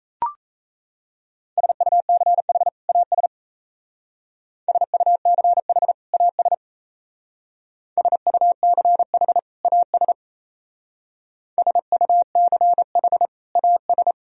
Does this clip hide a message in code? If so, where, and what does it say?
Yes: Morse code from 1.6 to 14.1 s, heard as a keyed tone, about 37 wpm, 700 Hz. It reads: SUCH AS SUCH AS SUCH AS SUCH AS